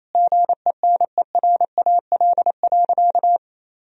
Morse code, heard as a keyed tone, about 27 wpm, 700 Hz. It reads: GENERAL.
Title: Sentences from Top 1000 Words - Review Part 3 - 28wpm